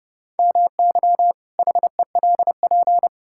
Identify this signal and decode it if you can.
Morse code, heard as a keyed tone, about 30 wpm, 700 Hz. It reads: MY HELP
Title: Pangram Challenge v2 - 30wpm